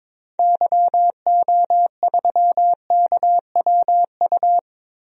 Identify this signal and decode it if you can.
Morse code, heard as a keyed tone, about 22 wpm, 700 Hz. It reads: YO3KWU
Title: DXCC-100 Call Signs - 6 Characters - Repeated 2x - 22wpm